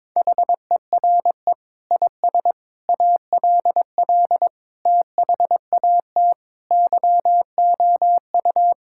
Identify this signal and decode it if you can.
Morse code, heard as a keyed tone, about 21 wpm, 700 Hz. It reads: HERE IS ALL THAT YOU
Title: Sentences from Top 1000 Words - Review Part 6 - 22wpm